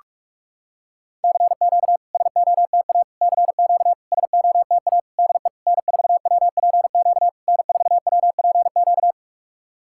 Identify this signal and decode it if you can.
Morse code, heard as a keyed tone, about 44 wpm, 700 Hz. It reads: CQ SOTA CQ SOTA DE N4WPQ N4WPQ
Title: SOTA QSO Practice 45wpm